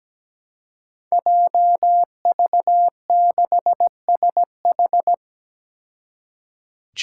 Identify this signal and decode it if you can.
Morse code, heard as a keyed tone, about 17 wpm, 700 Hz. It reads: JV6SH
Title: Five Challenging Letter-Number Combinations 17wpm